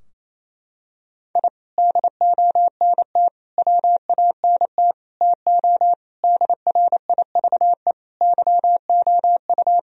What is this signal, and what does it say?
Morse code, heard as a keyed tone, about 28 wpm, 700 Hz. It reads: I DONT WANT TO DRIVE YOU
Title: Sentences from Top 1000 Words - Review Part 2 - 28wpm